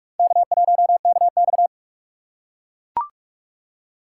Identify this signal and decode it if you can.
Morse code, heard as a keyed tone, about 44 wpm, 690 Hz. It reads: K1KX
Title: USA Call Signs - 4 Characters - 3x Character Spacing - 45wpm